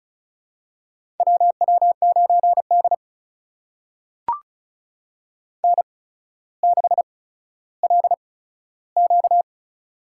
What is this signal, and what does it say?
Morse code, heard as a keyed tone, about 33 wpm, spaced as for 15 wpm, 700 Hz. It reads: WW9D N6LQ
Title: USA Call Signs - 4 Characters - 8x Character Spacing - 35wpm